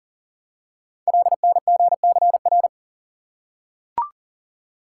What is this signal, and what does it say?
Morse code, heard as a keyed tone, about 42 wpm, 690 Hz. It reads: LNGCR
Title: Code Groups - 5 Letters - 40wpm